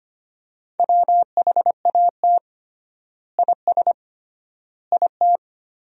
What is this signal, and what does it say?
Morse code, heard as a keyed tone, about 25 wpm, 700 Hz. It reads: WHAT IS IT
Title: 3x Word-Spacing - Sentences from Top 100 Words - 25wpm